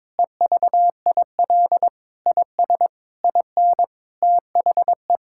Morse code, heard as a keyed tone, about 22 wpm, 700 Hz. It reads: EVIL IS IN THE
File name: Sentences with 9-Word Phrases - 22wpm